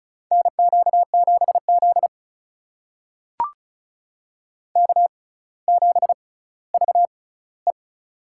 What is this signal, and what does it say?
Morse code, heard as a keyed tone, about 36 wpm, spaced as for 18 wpm, 700 Hz. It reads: NQ7Z K7VE
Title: USA Call Signs - 4 Characters - 6x Character Spacing - 35wpm